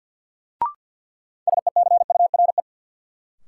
Morse code, heard as a keyed tone, about 50 wpm, 700 Hz. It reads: SECURE
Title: Sentences from Top 1200 Words - New Word Drill Alt - 50wpm